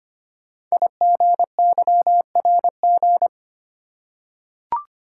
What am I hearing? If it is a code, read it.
Morse code, heard as a keyed tone, about 25 wpm, 700 Hz. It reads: IGYRG
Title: Code Groups - 5 Letters - 25wpm